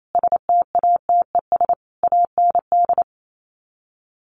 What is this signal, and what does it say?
Morse code, heard as a keyed tone, about 28 wpm, 700 Hz. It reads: STATES AND
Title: Sets of 5 Words 28wpm